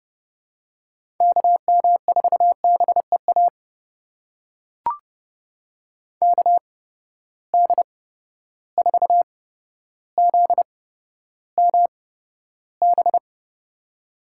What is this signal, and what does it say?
Morse code, heard as a keyed tone, about 30 wpm, spaced as for 13 wpm, 700 Hz. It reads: KM4BEA KD4ZMB
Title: USA Call Signs - 6 Characters - 8x Character Spacing - 30wpm